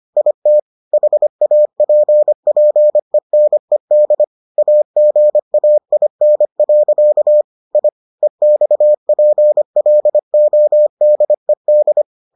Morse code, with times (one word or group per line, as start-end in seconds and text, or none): 0.2-0.6 IT
0.9-4.2 HAPPENED
4.6-7.4 AGAIN.
7.8-7.9 I
8.2-12.0 EXPLODED